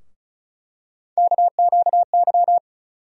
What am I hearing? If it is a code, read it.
Morse code, heard as a keyed tone, about 36 wpm, 700 Hz. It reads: KQY
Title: USA Call Sign Postfixes - 3 Letters 35wpm